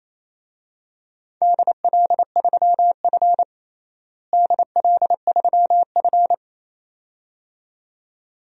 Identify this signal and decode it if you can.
Morse code, heard as a keyed tone, about 28 wpm, 700 Hz. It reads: DL3F DL3F
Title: DXCC-100 Call Signs - 4 Characters - Repeated 2x - 28wpm